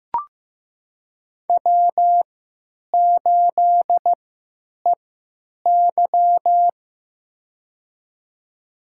Morse code, heard as a keyed tone, about 15 wpm, spaced as for 9 wpm, 700 Hz. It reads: W8EY
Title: USA Call Signs - 4 Characters - 3x Character Spacing - 15wpm